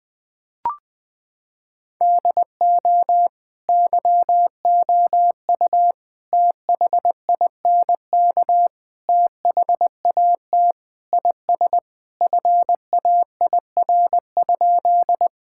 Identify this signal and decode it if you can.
Morse code, heard as a keyed tone, about 20 wpm, 700 Hz. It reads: DO YOU THINK THAT IS FAIR?
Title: Sentences with 4-Word Phrases - 20wpm